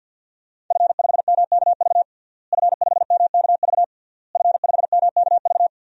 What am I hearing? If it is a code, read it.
Morse code, heard as a keyed tone, about 51 wpm, 700 Hz. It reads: F5MKV F5MKV F5MKV
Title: DXCC-100 Call Signs - 5 Characters - Repeated 3x - 50wpm